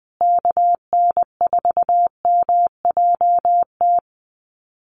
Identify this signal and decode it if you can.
Morse code, heard as a keyed tone, about 20 wpm, 700 Hz. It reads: KN4MJT
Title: USA Call Signs Repeated 3x 20wpm